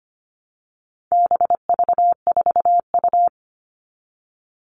Morse code, heard as a keyed tone, about 25 wpm, 700 Hz. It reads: BV4U
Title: All Cut Numbers - 4-Digit 25wpm